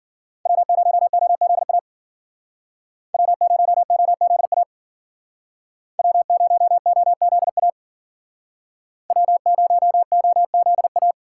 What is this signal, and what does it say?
Morse code, heard as a keyed tone, about 50 wpm, 700 Hz. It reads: W0OZA W0OZA W0OZA W0OZA
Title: Speed Racing - USA 1x3 Call Signs 40wpm